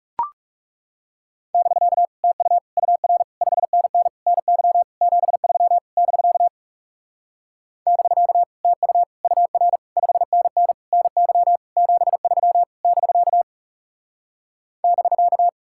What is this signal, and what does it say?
Morse code, heard as a keyed tone, about 39 wpm, 700 Hz. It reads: <BK> TU UR 5NN NY 73 <BK> <BK> TU UR 5NN NY 73 <BK> <BK>